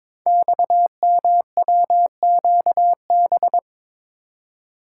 Morse code, heard as a keyed tone, about 22 wpm, 700 Hz. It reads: XMWQB
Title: Code Groups - 5 Letter:Numbers - 22wpm